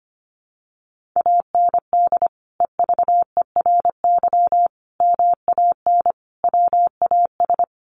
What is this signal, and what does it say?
Morse code, heard as a keyed tone, about 25 wpm, 700 Hz. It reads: AND EVERY MAN WAS
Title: Sentences from Top 1500 Words 25wpm